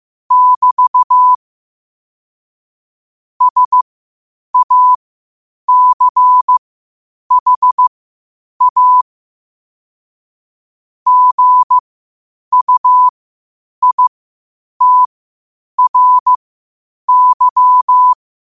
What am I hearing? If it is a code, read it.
Morse code, heard as a keyed tone, about 15 wpm, spaced as for 8 wpm, 1000 Hz. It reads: <BT> SACHA GUITRY
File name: qotdfiles / 8wpm